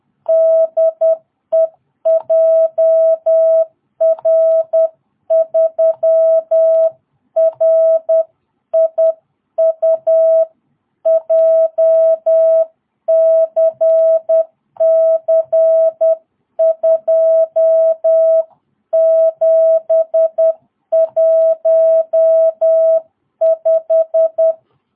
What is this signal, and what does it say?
Morse code, heard as a keyed tone, about 10 wpm, 660 Hz. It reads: DE JR3RIU JCC2715
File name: jh8cbh / 2022cw